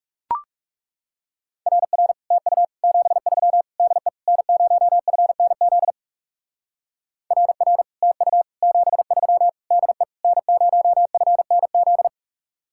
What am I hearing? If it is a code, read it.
Morse code, heard as a keyed tone, about 39 wpm, 700 Hz. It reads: RR TU 73 DE N0FNZ RR TU 73 DE N0FNZ